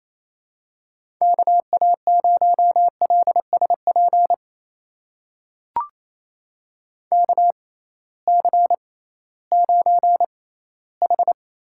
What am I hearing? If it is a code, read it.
Morse code, heard as a keyed tone, about 28 wpm, spaced as for 16 wpm, 700 Hz. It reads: KA0LSP KC9H
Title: USA Call Signs - 6 Characters - 6x Character Spacing - 28wpm